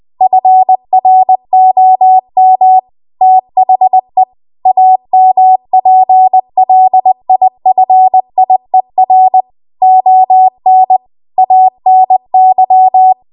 Morse code, heard as a keyed tone, about 20 wpm, 750 Hz. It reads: FROM THE AMPLIFIER ON ANY
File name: Archive / 20 WPM / 20WPM